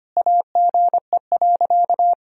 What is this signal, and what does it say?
Morse code, heard as a keyed tone, about 25 wpm, 700 Hz. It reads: AGE.
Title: Sentences from Top 1000 Words - Review Part 3 - 25wpm